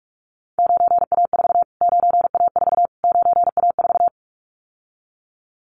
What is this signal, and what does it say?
Morse code, heard as a keyed tone, about 46 wpm, 700 Hz. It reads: 9A4 9A4 9A4
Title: Callsign Prefixes for Top 25 DXCC - Repeated 3x - 45wpm